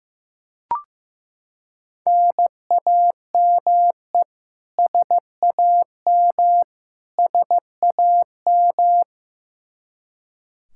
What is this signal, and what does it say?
Morse code, heard as a keyed tone, about 15 wpm, 700 Hz. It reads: NAME SAM SAM
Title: Practice QSO - Exchange 1 - Beginning - 15wpm